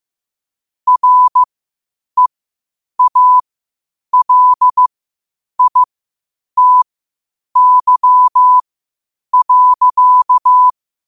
Morse code, heard as a keyed tone, about 15 wpm, spaced as for 8 wpm, 1000 Hz. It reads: REALITY.
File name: qotdfiles / 8wpm